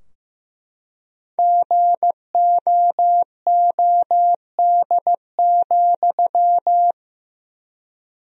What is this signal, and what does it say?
Morse code, heard as a keyed tone, about 15 wpm, 700 Hz. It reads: GOOD,
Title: 3x Word-Spacing - Sentences from Top 200 Words - 15wpm